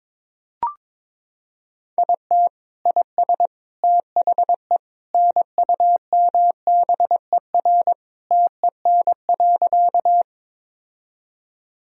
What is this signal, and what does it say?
Morse code, heard as a keyed tone, about 21 wpm, 700 Hz. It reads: IT IS THE NUMBER TEN.
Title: Sentences from Top 1000 Words - Review Part 2 - 22wpm